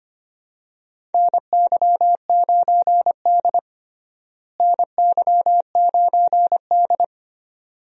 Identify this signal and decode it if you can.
Morse code, heard as a keyed tone, about 25 wpm, 700 Hz. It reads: NY9D NY9D
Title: USA Call Signs - 4 Characters - Repeated 2x - 25wpm